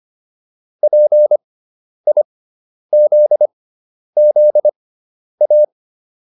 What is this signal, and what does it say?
Morse code, heard as a keyed tone, about 25 wpm, spaced as for 10 wpm, 600 Hz. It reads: PIZZA